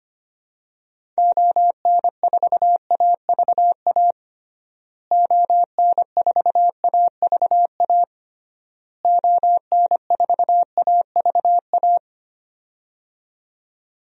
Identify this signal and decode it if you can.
Morse code, heard as a keyed tone, about 25 wpm, 700 Hz. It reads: ON4AVA ON4AVA ON4AVA
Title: DXCC-100 Call Signs - 6 Characters - Repeated 3x - 25wpm